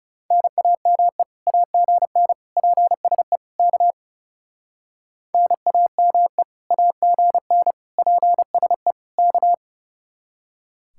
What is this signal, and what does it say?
Morse code, heard as a keyed tone, about 36 wpm, 700 Hz. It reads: NAME AGN PSE K NAME AGN PSE K
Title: Speed-Racing Common QSO Phrases Encore - 30wpm